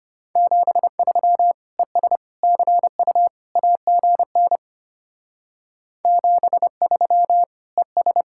Morse code, heard as a keyed tone, about 30 wpm, 700 Hz. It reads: 73 ES CU AGN 73 ES
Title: Speed-Racing Common QSO Phrases - 25wpm